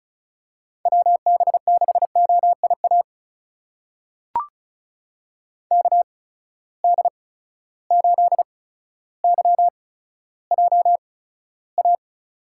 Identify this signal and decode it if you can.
Morse code, heard as a keyed tone, about 36 wpm, spaced as for 15 wpm, 700 Hz. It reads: WB6OIA KD8YJA
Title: USA Call Signs - 8x Character Spacing - 35wpm